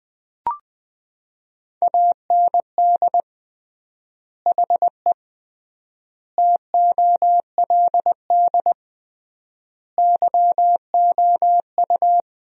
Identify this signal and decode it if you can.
Morse code, heard as a keyed tone, about 20 wpm, 700 Hz. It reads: AND HE TOLD YOU
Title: 3x Word-Spacing - Sentences from Top 500 Words - 20wpm